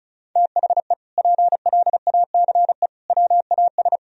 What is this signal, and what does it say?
Morse code, heard as a keyed tone, about 34 wpm, 700 Hz. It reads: THE PLACE WAS
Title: Sentences from Top 1000 Words - Review Part 1 - 35wpm